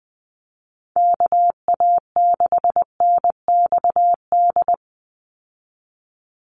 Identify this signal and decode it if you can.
Morse code, heard as a keyed tone, about 20 wpm, 700 Hz. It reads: KA6NXD
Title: USA Call Signs 20wpm